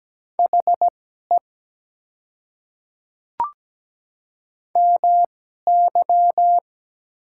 Morse code, heard as a keyed tone, about 17 wpm, spaced as for 12 wpm, 700 Hz. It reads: HE MY
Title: 2x Character Spacing - Top 300 Words - 17wpm